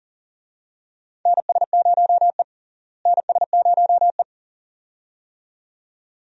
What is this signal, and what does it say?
Morse code, heard as a keyed tone, about 40 wpm, 700 Hz. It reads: NS0E NS0E
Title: USA Call Signs - 4 Characters - Repeated 2x - 40wpm